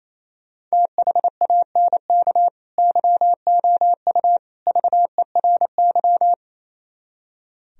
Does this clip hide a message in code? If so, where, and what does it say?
Yes: Morse code from 0.7 to 6.3 s, heard as a keyed tone, about 36 wpm, 700 Hz. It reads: THANK YOU VERY